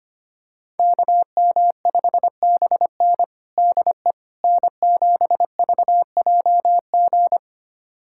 Morse code, heard as a keyed tone, about 25 wpm, 700 Hz. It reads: KM5BN DE N7VJG
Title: USA Callsign DE Callsign - 5 Characters - 25wpm